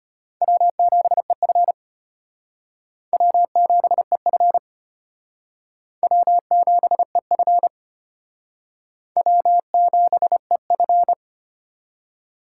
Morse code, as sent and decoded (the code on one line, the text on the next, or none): .-- --... ..-. / .-- --... ..-. / .-- --... ..-. / .-- --... ..-.
W7EF W7EF W7EF W7EF